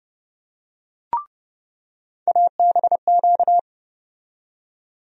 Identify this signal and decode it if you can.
Morse code, heard as a keyed tone, about 30 wpm, 700 Hz. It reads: ABQ
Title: USA Call Sign Postfixes - 3 Letters 30wpm